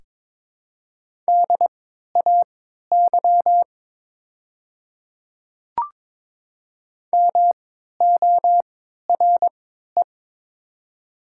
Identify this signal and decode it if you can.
Morse code, heard as a keyed tone, about 23 wpm, spaced as for 13 wpm, 700 Hz. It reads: DAY MORE
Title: 3x Character Spacing - Top 100 Words - 22wpm